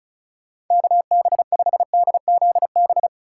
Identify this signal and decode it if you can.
Morse code, heard as a keyed tone, about 43 wpm, 700 Hz. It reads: KB5DZB
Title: Speed Racing - USA 2x3 Call Signs 35wpm